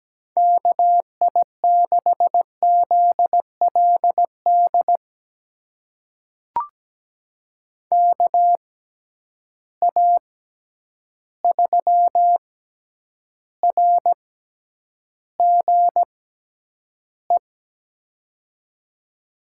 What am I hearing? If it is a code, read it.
Morse code, heard as a keyed tone, about 17 wpm, spaced as for 9 wpm, 700 Hz. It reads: KI6ZLD KA3RGE
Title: USA Call Signs - 6 Characters - 6x Character Spacing - 17wpm